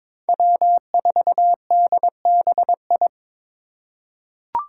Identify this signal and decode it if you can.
Morse code, heard as a keyed tone, about 21 wpm, 700 Hz. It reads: W4DBI
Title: Five Challenging Letter-Number Combinations 22wpm